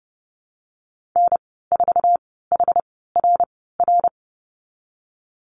Morse code, heard as a keyed tone, about 31 wpm, spaced as for 17 wpm, 700 Hz. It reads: N4HRR
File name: Lesson 19g - Letter w - Call Signs 30wpm